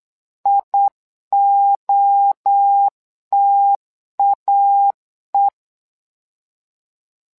Morse code, heard as a keyed tone, about 8 wpm, 800 Hz. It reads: IOTAE
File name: LEZIONI / 3 MWRO / 3 MWRO 8wpm 2606